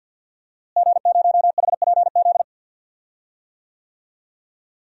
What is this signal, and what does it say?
Morse code, heard as a keyed tone, about 52 wpm, 700 Hz. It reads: G0HPZ
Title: DXCC-100 Call Signs - 5 Characters - Repeated 2x - 50wpm